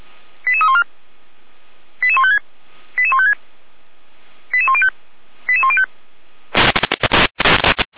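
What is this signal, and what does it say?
Five-tone selective call (ZVEI), heard as quick runs of five tones, tones of about 70 ms. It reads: 88314 88156 88156 88105 88105
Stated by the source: ZVEI five-tone selective call received on an SDR, archived by Feuerwehr Orlamünde (f85360).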